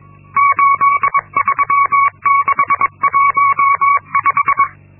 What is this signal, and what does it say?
Morse code, heard as a keyed tone, about 22 wpm, 1120 Hz. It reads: 83615
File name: Linksmakalnis-2